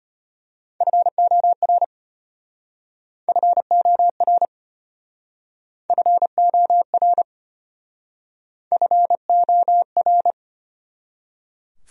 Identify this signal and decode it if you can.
Morse code, heard as a keyed tone, about 34 wpm, 700 Hz. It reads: FOR FOR FOR FOR